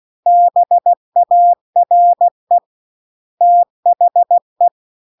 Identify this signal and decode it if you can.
Morse code, heard as a keyed tone, about 16 wpm, 700 Hz. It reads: BARE THE
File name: CW-The-Big-Twoheaded-River-2-16WPM